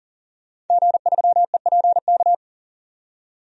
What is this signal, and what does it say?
Morse code, heard as a keyed tone, about 40 wpm, 700 Hz. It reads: G3EPK
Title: DXCC-100 Call Signs - 5 Characters - 40wpm